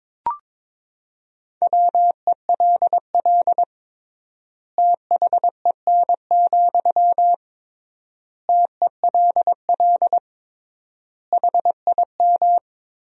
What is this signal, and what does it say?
Morse code, heard as a keyed tone, about 22 wpm, 700 Hz. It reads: WELL THEN, TELL HIM
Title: 3x Word-Spacing - Sentences from Top 200 Words - 22wpm